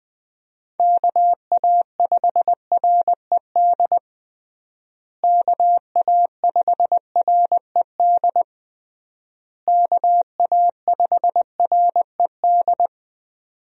Morse code, heard as a keyed tone, about 21 wpm, 700 Hz. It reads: KA5RED KA5RED KA5RED
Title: USA Call Signs - 6 Characters - Repeated 3x - 20wpm